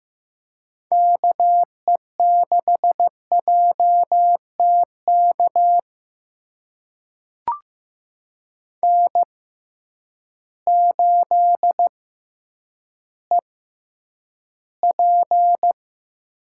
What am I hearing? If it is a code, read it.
Morse code, heard as a keyed tone, about 15 wpm, spaced as for 7 wpm, 700 Hz. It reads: KE6JT KN8EP
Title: USA Call Signs - 6x Character Spacing - 15wpm